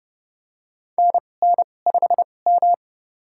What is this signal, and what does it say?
Morse code, heard as a keyed tone, about 30 wpm, spaced as for 22 wpm, 690 Hz. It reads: NN5M
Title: USA Call Signs - 4 Characters - 2x Character Spacing - 30wpm